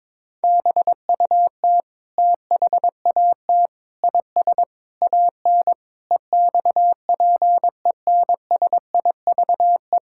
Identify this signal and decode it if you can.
Morse code, heard as a keyed tone, about 22 wpm, 700 Hz. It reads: BUT THAT IS AN EXPENSIVE
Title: Sentences from Top 1000 Words - Review Part 6 - 22wpm